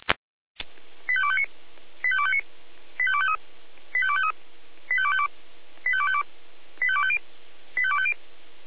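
Five-tone selective call (ZVEI), heard as quick runs of five tones, tones of about 70 ms. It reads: 85259 85259 85253 85253 85252 85252 85250 85250
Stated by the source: ZVEI five-tone selective call received via SDR.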